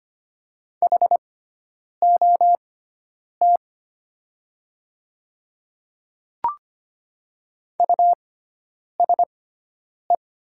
Morse code, heard as a keyed tone, about 25 wpm, spaced as for 9 wpm, 700 Hz. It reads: HOT USE
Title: 6x Character Spacing - Top 100 Words - 25wpm